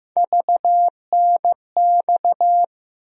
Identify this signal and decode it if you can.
Morse code, heard as a keyed tone, about 16 wpm, 700 Hz. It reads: VNX
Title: USA Call Sign Postfixes - 3 Letters 15wpm